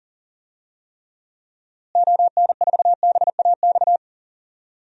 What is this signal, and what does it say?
Morse code, heard as a keyed tone, about 40 wpm, 700 Hz. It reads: ON4BAX
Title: DXCC-100 Call Signs - 6 Characters - Repeated 2x - 40wpm